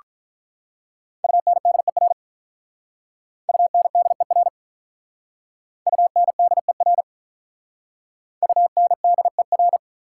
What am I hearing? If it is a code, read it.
Morse code, heard as a keyed tone, about 43 wpm, 690 Hz. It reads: UNDER UNDER UNDER UNDER
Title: Speed-Racing Prefixes and Suffixes 35wpm